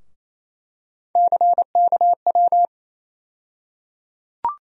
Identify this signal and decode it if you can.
Morse code, heard as a keyed tone, about 28 wpm, 700 Hz. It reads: CKW
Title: USA Call Sign Postfixes - 3 Letters 28wpm